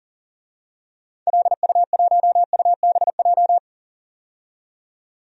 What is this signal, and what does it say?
Morse code, heard as a keyed tone, about 40 wpm, 700 Hz. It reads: LU1UBJ
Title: DXCC-100 Call Signs - 6 Characters - 40wpm